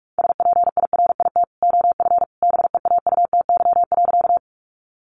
Morse code, heard as a keyed tone, about 44 wpm, 700 Hz. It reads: SPIRIT OF BEAUTY.
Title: Sentences from Top 1400 Words 45wpm